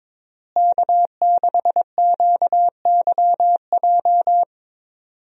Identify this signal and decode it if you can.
Morse code, heard as a keyed tone, about 21 wpm, 700 Hz. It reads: K6QYJ
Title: USA Call Signs - 5 Characters - Repeated 3x - 22wpm